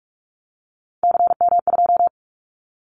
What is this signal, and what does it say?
Morse code, heard as a keyed tone, about 46 wpm, 700 Hz. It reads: CM2
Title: Callsign Prefixes for Top 100 DXCC - 45wpm